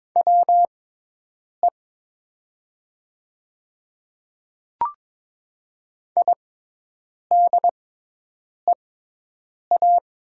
Morse code, heard as a keyed tone, about 22 wpm, spaced as for 8 wpm, 700 Hz. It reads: WE IDEA
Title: 6x Character Spacing - Top 300 Words - 22wpm